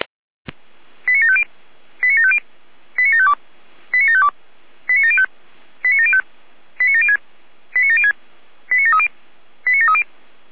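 Five-tone selective call (ZVEI), heard as quick runs of five tones, tones of about 70 ms. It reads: 89750 89750 89752 89752 89705 89705 89706 89706 89730 89730
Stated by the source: ZVEI five-tone selective call received by SDR.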